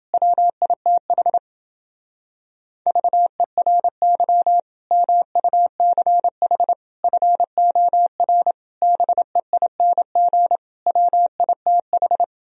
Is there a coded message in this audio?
Yes: Morse code from 0.1 to 12.2 s, heard as a keyed tone, about 29 wpm, 700 Hz. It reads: WITH VERY MUCH FOR BEING WITH